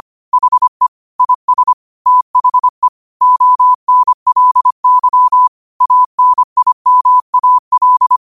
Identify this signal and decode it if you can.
Morse code, heard as a keyed tone, about 25 wpm, 1000 Hz. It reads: HE IS THE ONLY ANIMAL